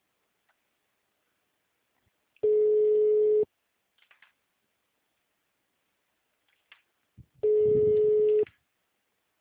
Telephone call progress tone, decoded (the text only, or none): ringback tone